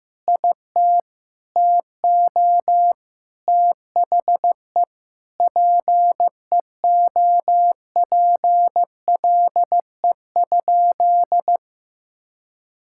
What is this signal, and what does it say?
Morse code, heard as a keyed tone, about 16 wpm, 700 Hz. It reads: IT TO THE PEOPLE?
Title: Sentences from Top 1000 Words - Review Part 5 - 15wpm